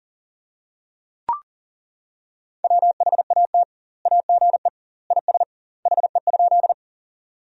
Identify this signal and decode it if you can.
Morse code, heard as a keyed tone, about 40 wpm, 700 Hz. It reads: WHAT AGE IS HE?
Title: 2x Word-Spacing - Sentences from Top 500 Words - 40wpm